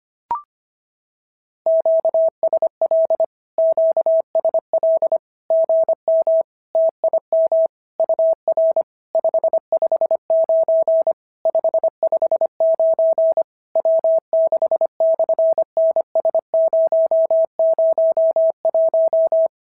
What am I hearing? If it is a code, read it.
Morse code, heard as a keyed tone, about 25 wpm, 650 Hz. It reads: QSL QSL GM TIM UR 559 559 W6/NS001
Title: SOTA QSO Practice 25wpm